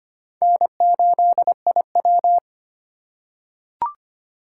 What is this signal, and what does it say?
Morse code, heard as a keyed tone, about 25 wpm, 700 Hz. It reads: N8IW
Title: USA Call Signs - 4 Characters - 4x Character Spacing - 25wpm